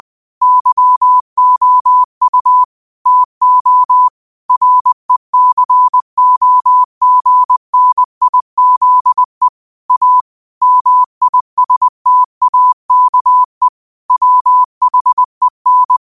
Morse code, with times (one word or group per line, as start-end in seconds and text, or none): 0.4-2.6 YOU
3.1-4.1 TO
4.5-9.5 RECOGNIZE
9.9-10.2 A
10.6-13.7 MISTAKE
14.1-16.0 WHEN